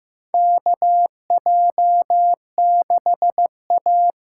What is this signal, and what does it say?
Morse code, heard as a keyed tone, about 15 wpm, 700 Hz. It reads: KJ6A
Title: USA Call Signs - 6 Characters - 6x Character Spacing - 15wpm